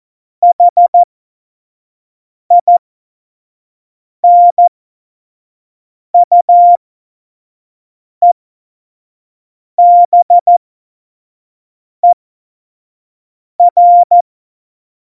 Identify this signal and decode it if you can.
Morse code, heard as a keyed tone, about 14 wpm, spaced as for 5 wpm, 700 Hz. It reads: HINUEBER
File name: Morsekurs / Witze / Witz 05 Mai 2020, 25 F70